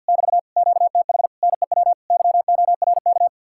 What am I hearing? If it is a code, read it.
Morse code, heard as a keyed tone, about 62 wpm, 700 Hz. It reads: <BT> QTH NEW YORK
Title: Speed-Racing Common QSO Phrases - 50wpm